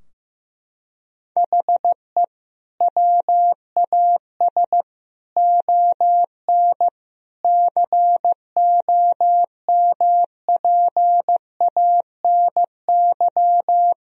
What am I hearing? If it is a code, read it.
Morse code, heard as a keyed tone, about 16 wpm, 700 Hz. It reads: HE WAS ON COMPANY
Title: Sentences from Top 1000 Words - Review Part 3 - 15wpm